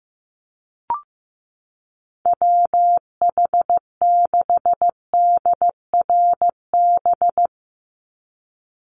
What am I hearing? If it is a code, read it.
Morse code, heard as a keyed tone, about 15 wpm, 700 Hz. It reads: WH6DRB
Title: USA Call Signs 15wpm